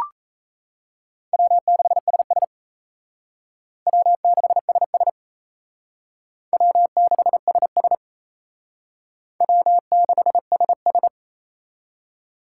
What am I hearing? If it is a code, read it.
Morse code, heard as a keyed tone, about 35 wpm, 700 Hz. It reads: W6SS W6SS W6SS W6SS
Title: Speed Racing - USA 1x2 Call Signs 28wpm